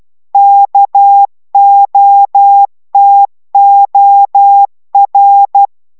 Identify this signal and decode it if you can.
Morse code, heard as a keyed tone, about 12 wpm, 800 Hz. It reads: KOTOR
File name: taviro tanulas / cw 69 15 10